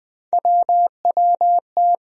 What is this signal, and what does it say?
Morse code, heard as a keyed tone, about 21 wpm, 700 Hz. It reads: WWT
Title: USA Call Sign Postfixes - 3 Letters 20wpm